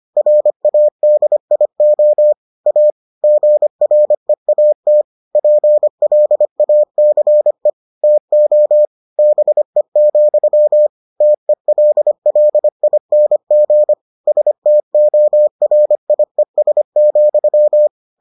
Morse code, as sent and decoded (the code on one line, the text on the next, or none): .-. .- -.. .. --- / .- / --. .-. .- - / .--. .-.. .- -.-. / - --- / -... --..-- / - . .-.. .-.. .. -. --. / ... - --- .-. .. ... --..--
RADIO A GREAT PLACE TO BE, TELLING STORIES,